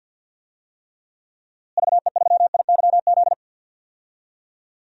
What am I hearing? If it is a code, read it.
Morse code, heard as a keyed tone, about 47 wpm, 700 Hz. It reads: VE3IYZ
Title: DXCC-100 Call Signs - 6 Characters - Repeated 3x - 50wpm